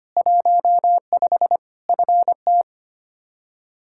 Morse code, heard as a keyed tone, about 25 wpm, 700 Hz. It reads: 15 FT
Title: Practice QSO - Exchange 2 - Info - 25wpm